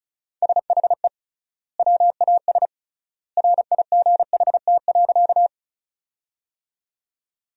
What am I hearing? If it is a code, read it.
Morse code, heard as a keyed tone, about 36 wpm, 700 Hz. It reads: SHE WAS RIGHT.
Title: 3x Word-Spacing - Sentences from Top 200 Words - 35wpm